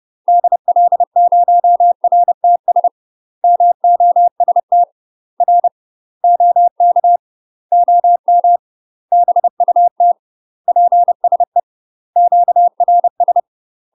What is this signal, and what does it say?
Morse code, heard as a keyed tone, about 30 wpm, 700 Hz. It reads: DL0RTS MOST R OK OM BUT PSE QRS
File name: UK MASTER QSOs TRAINING / 30WPM / UK MASTER QSOs 0044